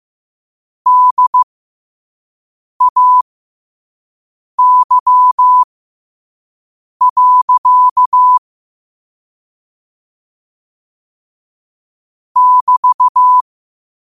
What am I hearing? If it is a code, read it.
Morse code, heard as a keyed tone, about 15 wpm, spaced as for 5 wpm, 1000 Hz. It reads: DAY. <BT>
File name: qotdfiles / 5wpm